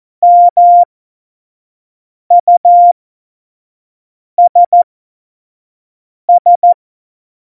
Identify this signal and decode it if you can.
Morse code, heard as a keyed tone, about 14 wpm, spaced as for 5 wpm, 700 Hz. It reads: MUSS
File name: Morsekurs / Witze / Witz 23. Januar 2018, 25 F70